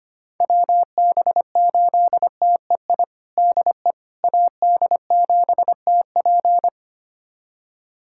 Morse code, heard as a keyed tone, about 25 wpm, 700 Hz. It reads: WB8TEI DE AD7TP
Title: USA Callsign DE Callsign 25wpm